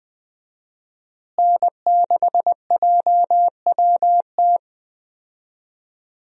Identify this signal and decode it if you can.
Morse code, heard as a keyed tone, about 20 wpm, 700 Hz. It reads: N6JWT
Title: USA Call Signs - 5 Characters - Repeated 2x - 20wpm